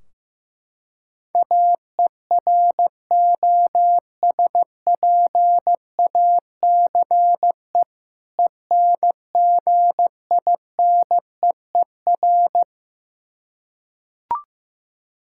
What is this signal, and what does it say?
Morse code, heard as a keyed tone, about 15 wpm, 700 Hz. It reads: AEROSPACE ENGINEER
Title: Practice QSO - Exchange 3 - More Info - 15wpm